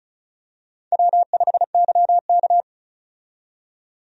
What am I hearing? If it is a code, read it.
Morse code, heard as a keyed tone, about 34 wpm, 700 Hz. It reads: W5YK
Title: USA Call Signs - 4 Characters - 35wpm